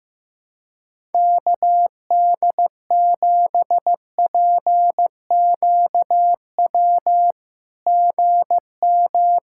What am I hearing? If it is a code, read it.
Morse code, heard as a keyed tone, about 15 wpm, 700 Hz. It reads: KD7PQW GM
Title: SOTA QSO Practice 15wpm